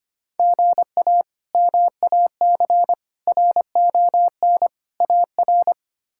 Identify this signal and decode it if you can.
Morse code, heard as a keyed tone, about 25 wpm, 700 Hz. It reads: GA MAC RON AR
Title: K1USN SST QSO Practice - 25wpm